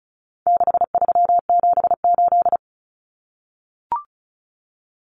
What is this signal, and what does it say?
Morse code, heard as a keyed tone, about 34 wpm, 700 Hz. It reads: 6378
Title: Numbers - 4-Digit 35wpm